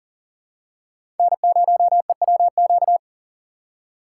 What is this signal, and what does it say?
Morse code, heard as a keyed tone, about 37 wpm, 700 Hz. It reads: N0EWQ
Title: USA Call Signs - 4x Character Spacing - 40wpm